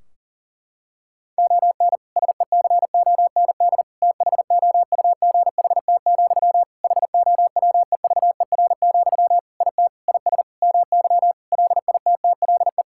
Morse code, heard as a keyed tone, about 40 wpm, 700 Hz. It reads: ON SECOND THOUGHT, HOWEVER, IT IS MY LITTLE
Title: Sentences from Top 1600 Words 40wpm